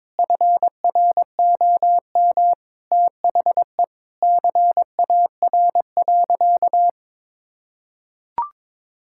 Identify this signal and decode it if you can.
Morse code, heard as a keyed tone, about 22 wpm, 700 Hz. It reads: FROM THE CAR.
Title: Sentences from Top 1000 Words - Review Part 5 - 22wpm